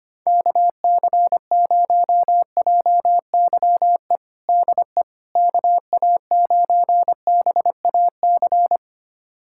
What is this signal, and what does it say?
Morse code, heard as a keyed tone, about 25 wpm, 700 Hz. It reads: KC0JYE DE KA9BAC
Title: USA Callsign DE Callsign - 6 Characters - 25wpm